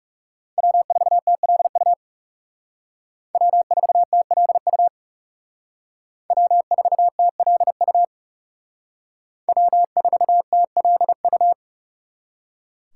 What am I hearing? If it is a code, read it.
Morse code, heard as a keyed tone, about 38 wpm, 700 Hz. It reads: W4TLU W4TLU W4TLU W4TLU